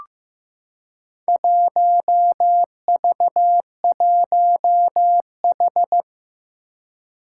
Morse code, heard as a keyed tone, about 15 wpm, 700 Hz. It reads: 1V1H
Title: Four Challenging Letter-Number Combinations 15wpm